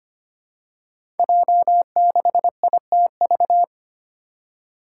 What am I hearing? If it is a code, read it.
Morse code, heard as a keyed tone, about 25 wpm, 700 Hz. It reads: J6ITV
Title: Five Challenging Letter-Number Combinations 25wpm